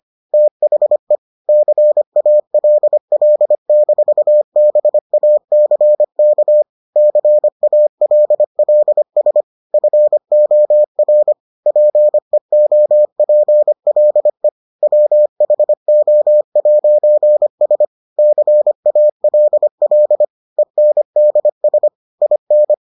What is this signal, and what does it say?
Morse code, heard as a keyed tone, about 25 wpm, 600 Hz. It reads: THE CALL-BACK CALLS FOR PEOPLE WHO'S CALL ENDS IN